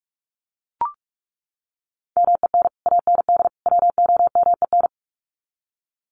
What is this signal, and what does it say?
Morse code, heard as a keyed tone, about 44 wpm, 700 Hz. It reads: MEN AND WOMEN
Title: Sets of 3 Words 45wpm